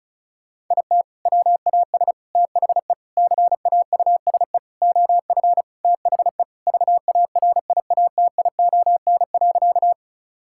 Morse code, heard as a keyed tone, about 34 wpm, 700 Hz. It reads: IT WAS THE CAUSE OF THE VARIATION.
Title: Sentences from Top 1600 Words 35wpm